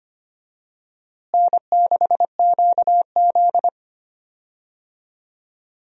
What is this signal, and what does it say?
Morse code, heard as a keyed tone, about 25 wpm, 700 Hz. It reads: N6QZ
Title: USA Call Signs - 4 Characters - Repeated 2x - 25wpm